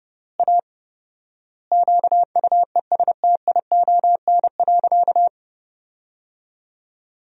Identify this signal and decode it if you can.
Morse code, heard as a keyed tone, about 30 wpm, 700 Hz. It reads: A QUESTION.
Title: 4x Word-Spacing - Sentences from Top 400 Words - 30wpm